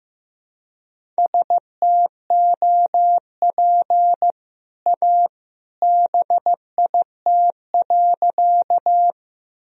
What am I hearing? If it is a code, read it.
Morse code, heard as a keyed tone, about 15 wpm, 700 Hz. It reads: STOP A BIT.
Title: Sentences from Top 1000 Words - Review Part 1 - 15wpm